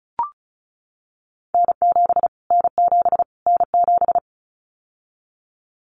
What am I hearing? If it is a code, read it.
Morse code, heard as a keyed tone, about 34 wpm, 700 Hz. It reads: N7 N7 N7
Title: Callsign Prefixes for Top 25 DXCC - Repeated 3x - 35wpm